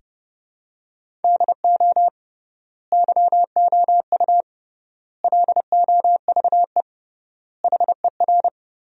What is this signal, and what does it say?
Morse code, heard as a keyed tone, about 30 wpm, 700 Hz. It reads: DO YOU LOVE HER
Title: 3x Word-Spacing - Sentences from Top 400 Words - 30wpm